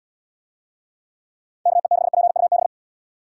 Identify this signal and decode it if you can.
Morse code, heard as a keyed tone, about 77 wpm, 700 Hz. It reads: KE6PUB